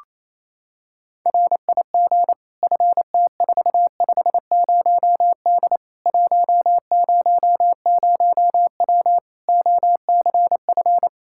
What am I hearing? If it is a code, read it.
Morse code, heard as a keyed tone, about 28 wpm, 700 Hz. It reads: RIG FT450D 100W OCF
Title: Practice QSO - Exchange 2 - Info - 28wpm